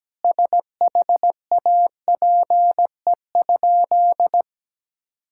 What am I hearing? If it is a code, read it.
Morse code, heard as a keyed tone, about 17 wpm, 700 Hz. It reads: SHAPE?
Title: Sentences from Top 1000 Words - Review Part 6 - 17wpm